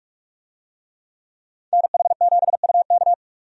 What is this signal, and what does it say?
Morse code, heard as a keyed tone, about 42 wpm, 700 Hz. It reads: NH7UK
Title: USA Call Signs - 5 Characters - Repeated 2x - 45wpm